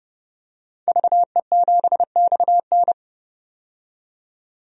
Morse code, heard as a keyed tone, about 29 wpm, 700 Hz. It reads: VE7XN
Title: DXCC-100 Call Signs - 5 Characters - 30wpm